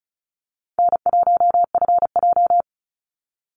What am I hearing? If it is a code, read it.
Morse code, heard as a keyed tone, about 34 wpm, 700 Hz. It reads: N1FJ
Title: NA SOTA Activator Callsigns 35wpm